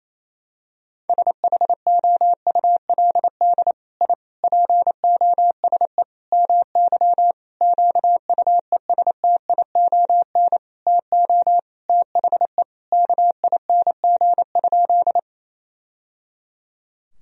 Morse code, heard as a keyed tone, about 28 wpm, 700 Hz. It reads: SHOULD I POSE MY QUESTION TO THE KING?
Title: Sentences from Top 1000 Words - Review Part 2 - 28wpm